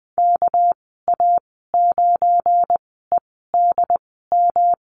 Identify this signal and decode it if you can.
Morse code, heard as a keyed tone, about 20 wpm, spaced as for 14 wpm, 700 Hz. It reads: KA9EDM
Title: 219 Lesson 35g - Number 0 - Call Signs 20wpm